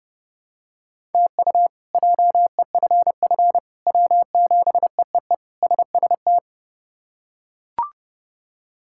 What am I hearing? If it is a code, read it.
Morse code, heard as a keyed tone, about 29 wpm, 700 Hz. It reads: TU JEFF W7EEE SST
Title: K1USN SST QSO Practice - 30wpm